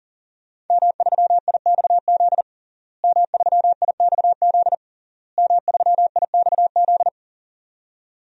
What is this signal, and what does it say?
Morse code, heard as a keyed tone, about 40 wpm, 700 Hz. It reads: M3IXZ M3IXZ M3IXZ
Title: DXCC-100 Call Signs - 5 Characters - Repeated 3x - 40wpm